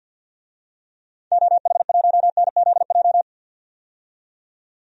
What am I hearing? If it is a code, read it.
Morse code, heard as a keyed tone, about 52 wpm, 700 Hz. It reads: OH1NZJ
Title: DXCC-100 Call Signs - 6 Characters - 50wpm